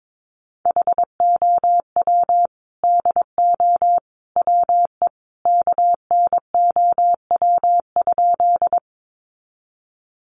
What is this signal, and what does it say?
Morse code, heard as a keyed tone, about 22 wpm, 700 Hz. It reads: HOW DO WE KNOW?
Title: A Sound Of Thunder 22wpm